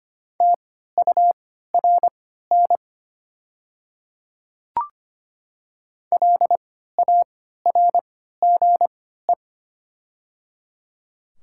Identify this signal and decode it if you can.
Morse code, heard as a keyed tone, about 25 wpm, spaced as for 14 wpm, 700 Hz. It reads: TURN LARGE
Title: 3x Character Spacing - Top 200 Words - 25wpm